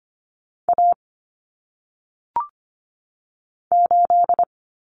Morse code, heard as a keyed tone, about 25 wpm, 700 Hz. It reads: A 8
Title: Common Cut Numbers - 1-Digit 25wpm